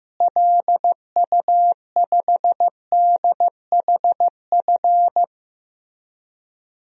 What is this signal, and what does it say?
Morse code, heard as a keyed tone, about 15 wpm, 700 Hz. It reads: LU5DHF
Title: DXCC-100 Call Signs - 6 Characters - Repeated 3x - 15wpm